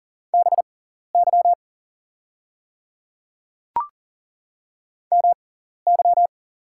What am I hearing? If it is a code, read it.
Morse code, heard as a keyed tone, about 42 wpm, spaced as for 14 wpm, 700 Hz. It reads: BY MY
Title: 6x Character Spacing - Top 100 Words - 40wpm